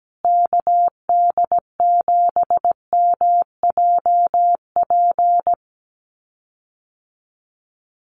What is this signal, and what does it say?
Morse code, heard as a keyed tone, about 17 wpm, 700 Hz. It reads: KD7MJP
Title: USA Call Signs Repeated 3x 17wpm